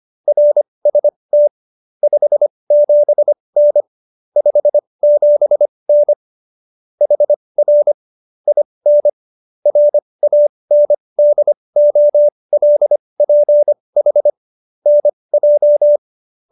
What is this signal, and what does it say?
Morse code, heard as a keyed tone, about 25 wpm, spaced as for 20 wpm, 590 Hz. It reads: RST 57N 57N HR IN RANDOLPH NJ
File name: QSO220 20